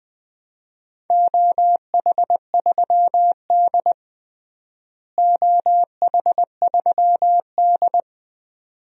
Morse code, heard as a keyed tone, about 20 wpm, 700 Hz. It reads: OH3D OH3D
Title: DXCC-100 Call Signs - 4 Characters - Repeated 2x - 20wpm